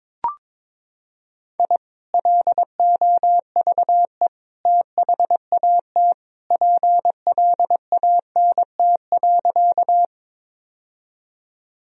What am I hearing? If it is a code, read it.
Morse code, heard as a keyed tone, about 21 wpm, 700 Hz. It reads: I LOVE THAT PLANT.
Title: Sentences from Top 1000 Words - Review Part 1 - 22wpm